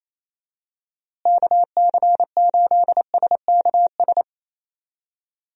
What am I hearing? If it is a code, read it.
Morse code, heard as a keyed tone, about 28 wpm, 700 Hz. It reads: KC8SKS
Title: USA Call Signs - 6 Characters - Repeated 3x - 28wpm